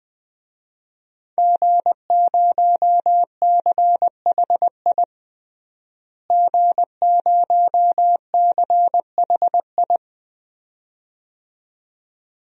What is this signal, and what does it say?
Morse code, heard as a keyed tone, about 21 wpm, 700 Hz. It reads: G0CHI G0CHI
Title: DXCC-100 Call Signs - 5 Characters - Repeated 2x - 20wpm